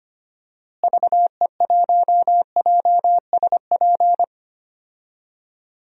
Morse code, heard as a keyed tone, about 25 wpm, 700 Hz. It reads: VE1JSP
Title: DXCC-100 Call Signs - 6 Characters - Repeated 2x - 25wpm